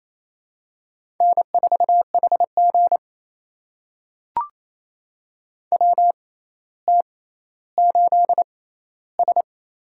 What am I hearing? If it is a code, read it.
Morse code, heard as a keyed tone, about 28 wpm, spaced as for 14 wpm, 700 Hz. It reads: N4HG WT8S
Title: USA Call Signs - 4 Characters - 6x Character Spacing - 28wpm